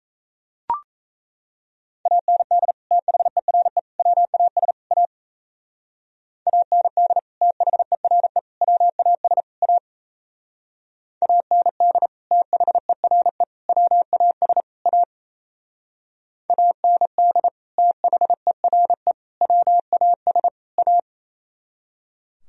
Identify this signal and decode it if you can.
Morse code, heard as a keyed tone, about 35 wpm, 700 Hz. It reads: AND THERE WAS A AND THERE WAS A AND THERE WAS A AND THERE WAS A
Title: Speed-Racing - Sets of 4 Words 28wpm